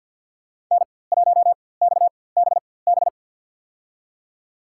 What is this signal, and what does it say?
Morse code, heard as a keyed tone, about 49 wpm, spaced as for 23 wpm, 700 Hz. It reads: N1XBB
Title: USA Call Signs - 5 Characters - 4x Character Spacing - 50wpm